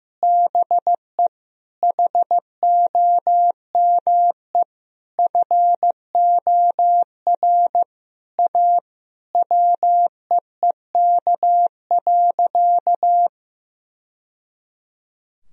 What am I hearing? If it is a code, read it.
Morse code, heard as a keyed tone, about 16 wpm, 700 Hz. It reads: BE HOME FOR A WEEK.
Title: Sentences from Top 1000 Words - Review Part 2 - 15wpm